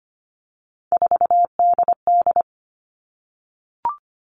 Morse code, heard as a keyed tone, about 25 wpm, 700 Hz. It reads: 4DD
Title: All Cut Numbers - 3-Digit 25wpm